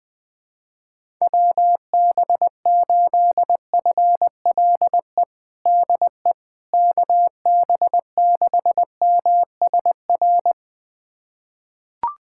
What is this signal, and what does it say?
Morse code, heard as a keyed tone, about 21 wpm, 700 Hz. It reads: WB8FLE DE KB6MSR
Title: USA Callsign DE Callsign - 6 Characters - 20wpm